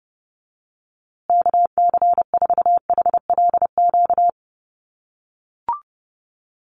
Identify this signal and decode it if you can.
Morse code, heard as a keyed tone, about 30 wpm, 700 Hz. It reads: KC4HLQ
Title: USA Call Signs Repeated 3x 30wpm